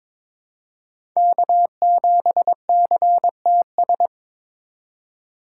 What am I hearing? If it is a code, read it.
Morse code, heard as a keyed tone, about 22 wpm, 700 Hz. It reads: K7CTS
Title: USA Call Signs - 5 Characters - 3x Character Spacing - 22wpm